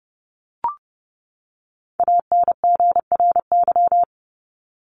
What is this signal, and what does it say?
Morse code, heard as a keyed tone, about 30 wpm, 700 Hz. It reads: ANGRY
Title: Top 1751-2000 Words 30wpm